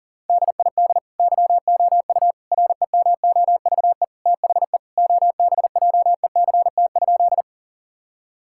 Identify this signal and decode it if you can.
Morse code, heard as a keyed tone, about 40 wpm, 700 Hz. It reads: DID YOU REMOVE THE OBJECT?
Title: Sentences from Top 1000 Words - Review Part 5 - 40wpm